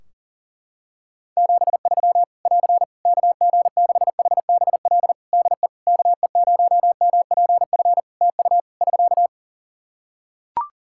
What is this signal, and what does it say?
Morse code, heard as a keyed tone, about 40 wpm, 700 Hz. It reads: 73 <AR> KG6HBL DE KE0MPF TU <SK>
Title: Practice QSO - Exchange 4 - Ending - 40wpm